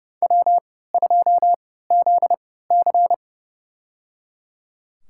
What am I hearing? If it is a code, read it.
Morse code, heard as a keyed tone, about 31 wpm, spaced as for 17 wpm, 700 Hz. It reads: W2ZC
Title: USA Call Signs - 4 Characters - 3x Character Spacing - 30wpm